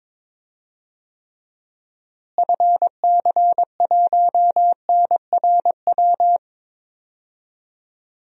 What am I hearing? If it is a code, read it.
Morse code, heard as a keyed tone, about 21 wpm, 700 Hz. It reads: FC1NRW
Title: DXCC-100 Call Signs - 6 Characters - Repeated 3x - 22wpm